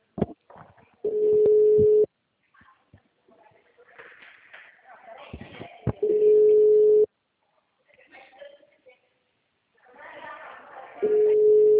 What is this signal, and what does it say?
Ringback tone (Europe, 425 Hz), with speech.